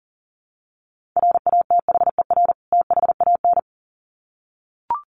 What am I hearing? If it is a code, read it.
Morse code, heard as a keyed tone, about 37 wpm, 700 Hz. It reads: RATHER THAN
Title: Sets of 2 Words 40wpm